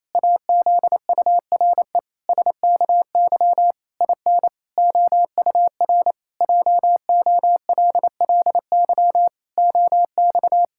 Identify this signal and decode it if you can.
Morse code, heard as a keyed tone, about 27 wpm, 700 Hz. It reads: AZURE SKY IN OUR JOLLY OX
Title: Pangram Challenge v2 - 28wpm